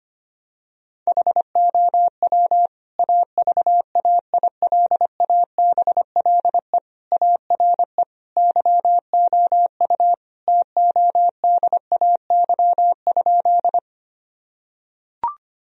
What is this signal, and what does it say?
Morse code, heard as a keyed tone, about 25 wpm, 700 Hz. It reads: HOW AVAILABLE ARE YOU TODAY?
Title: Sentences from Top 1000 Words - Review Part 3 - 25wpm